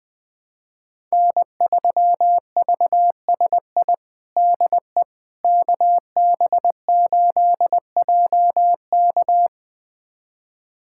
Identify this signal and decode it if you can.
Morse code, heard as a keyed tone, about 20 wpm, 700 Hz. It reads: N3VSI DE KB8JK
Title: USA Callsign DE Callsign - 5 Characters - 20wpm